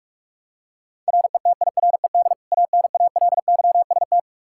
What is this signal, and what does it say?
Morse code, heard as a keyed tone, about 46 wpm, 700 Hz. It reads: RETIRED ANALYST
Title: Practice QSO - Exchange 3 - More Info - 45wpm